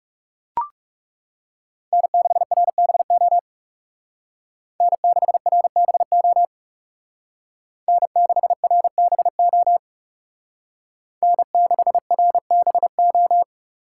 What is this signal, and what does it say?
Morse code, heard as a keyed tone, about 35 wpm, 700 Hz. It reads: N6RBO N6RBO N6RBO N6RBO
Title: Speed Racing - USA Call Signs 30wpm